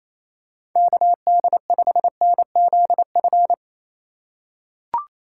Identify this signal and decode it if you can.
Morse code, heard as a keyed tone, about 27 wpm, 700 Hz. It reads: KD5NZF
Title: USA Call Signs - 6 Characters - 4x Character Spacing - 28wpm